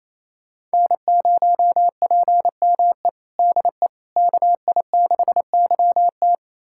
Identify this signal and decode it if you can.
Morse code, heard as a keyed tone, about 27 wpm, 700 Hz. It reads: N0PME DE KI6YT
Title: USA Callsign DE Callsign - 5 Characters - 28wpm